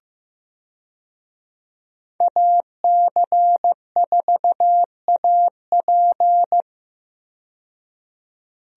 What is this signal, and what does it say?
Morse code, heard as a keyed tone, about 15 wpm, 700 Hz. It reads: AC4AP
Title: USA Call Signs - 5 Characters - 15wpm